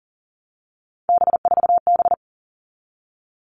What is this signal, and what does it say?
Morse code, heard as a keyed tone, about 39 wpm, 700 Hz. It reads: B4B
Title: All Cut Numbers - 3-Digit 40wpm